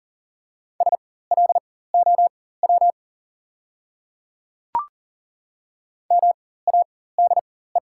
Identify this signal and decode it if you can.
Morse code, heard as a keyed tone, about 40 wpm, spaced as for 19 wpm, 690 Hz. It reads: SLOW MADE
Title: 4x Character Spacing - Top 500 Words - 40wpm